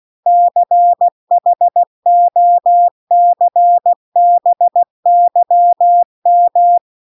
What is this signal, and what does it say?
Morse code, heard as a keyed tone, about 16 wpm, 700 Hz. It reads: CHOCBYM